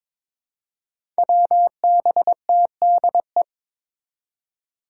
Morse code, heard as a keyed tone, about 22 wpm, 700 Hz. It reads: WBTDE
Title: Five Challenging Letter-Number Combinations 22wpm